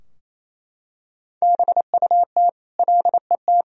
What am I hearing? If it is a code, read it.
Morse code, heard as a keyed tone, about 28 wpm, 700 Hz. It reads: BUT LET
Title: Sentences from Top 1000 Words - Review Part 4 - 28wpm